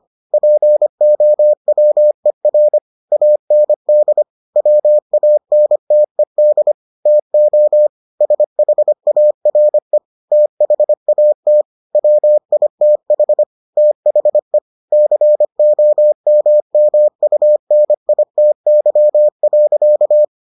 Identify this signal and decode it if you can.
Morse code, heard as a keyed tone, about 25 wpm, 600 Hz. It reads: POWER AND WANTED TO SHARE THAT WITH THE COMMUNITY.